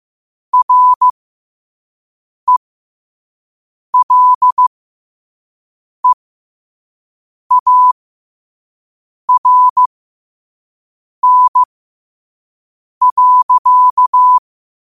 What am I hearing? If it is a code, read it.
Morse code, heard as a keyed tone, about 15 wpm, spaced as for 5 wpm, 1000 Hz. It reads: RELEARN.
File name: qotdfiles / 5wpm